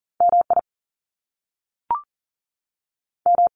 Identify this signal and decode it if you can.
Morse code, heard as a keyed tone, about 42 wpm, 700 Hz. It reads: MI M
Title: US States 40wpm